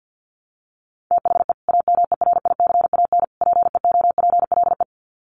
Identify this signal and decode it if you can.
Morse code, heard as a keyed tone, about 50 wpm, 700 Hz. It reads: THE AMERICAN PEOPLE
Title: Sets of 3 Words 50wpm